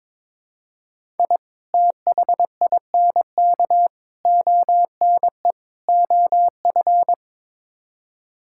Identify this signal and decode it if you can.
Morse code, heard as a keyed tone, about 21 wpm, 700 Hz. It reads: I THINK ONE OF
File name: practice / 22wpm / Build A Phrase - 5-Word Phrases - 22wpm